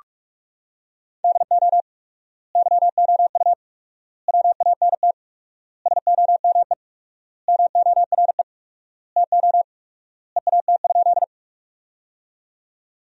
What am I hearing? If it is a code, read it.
Morse code, heard as a keyed tone, about 48 wpm, 700 Hz. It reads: DO YOU WANT SOME MORE TO EAT?
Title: 4x Word-Spacing - Sentences from Top 300 Words - 45wpm